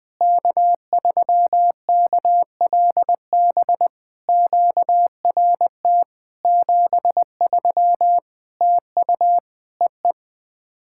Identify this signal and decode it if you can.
Morse code, heard as a keyed tone, about 20 wpm, 700 Hz. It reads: K3KLB QRT 73 TU EE
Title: POTA QSO Practice - 20wpm